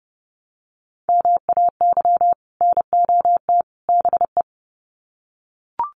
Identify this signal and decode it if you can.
Morse code, heard as a keyed tone, about 30 wpm, 700 Hz. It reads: MAY NOT BE
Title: Sets of 3 Words 30wpm